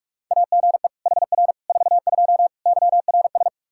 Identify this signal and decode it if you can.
Morse code, heard as a keyed tone, about 46 wpm, 700 Hz. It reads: AGE HR 42 YRS
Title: Practice QSO - Exchange 3 - More Info - 45wpm